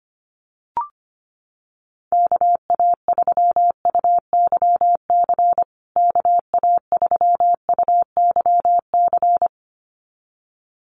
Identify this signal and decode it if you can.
Morse code, heard as a keyed tone, about 25 wpm, 700 Hz. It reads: KA3UYC KA3UYC
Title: USA Call Signs Repeated 2x 25wpm